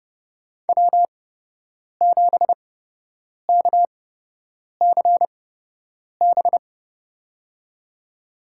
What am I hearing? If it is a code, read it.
Morse code, heard as a keyed tone, about 30 wpm, spaced as for 8 wpm, 700 Hz. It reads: W7KCB
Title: USA Call Signs - 5 Characters - 8x Character Spacing - 30wpm